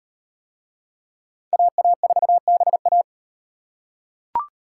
Morse code, heard as a keyed tone, about 38 wpm, 700 Hz. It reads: AA4BA